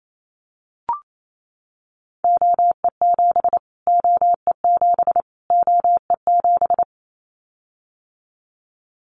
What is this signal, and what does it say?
Morse code, heard as a keyed tone, about 28 wpm, 700 Hz. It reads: OE7 OE7 OE7
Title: Callsign Prefixes for Top 25 DXCC - Repeated 3x - 28wpm